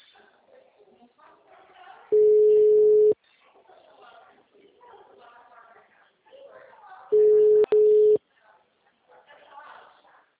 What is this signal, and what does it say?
Ringback tone (Europe, 425 Hz), with speech.